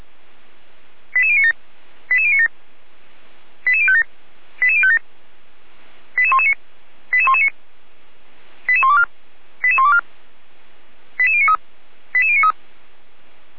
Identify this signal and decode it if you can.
Five-tone selective call (ZVEI), heard as quick runs of five tones, tones of about 70 ms. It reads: 88097 88097 88057 88057 88119 88119 88124 88124 88093 88093